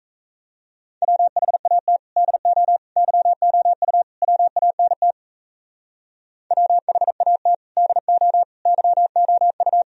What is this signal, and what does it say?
Morse code, heard as a keyed tone, about 33 wpm, 700 Hz. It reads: WHAT DO YOU WANT WHAT DO YOU
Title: Speed-Racing - Sets of 4 Words 28wpm